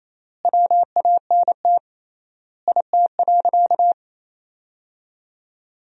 Morse code, heard as a keyed tone, about 28 wpm, 700 Hz. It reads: WANT IT.
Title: 3x Word-Spacing - Sentences from Top 300 Words - 28wpm